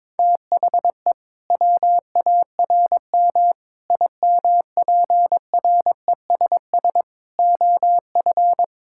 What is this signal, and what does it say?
Morse code, heard as a keyed tone, about 22 wpm, 700 Hz. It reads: THE WARM IMPRESS OF
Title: Sentences from Top 1600 Words 22wpm